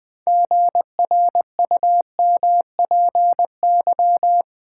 Morse code, heard as a keyed tone, about 20 wpm, 700 Hz. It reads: GRUMPY
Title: Pangram Challenge v2 - 20wpm